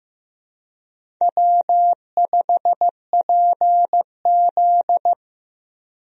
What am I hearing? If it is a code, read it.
Morse code, heard as a keyed tone, about 15 wpm, 700 Hz. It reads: W5PZ
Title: USA Call Signs - 4 Characters - Repeated 3x - 15wpm